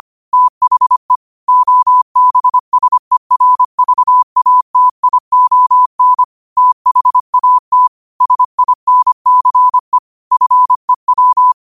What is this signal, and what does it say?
Morse code, heard as a keyed tone, about 25 wpm, 1000 Hz. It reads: THE OBSERVATION THAT SINCE FEW
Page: New quotes every day in morse code at 25 Words per minute.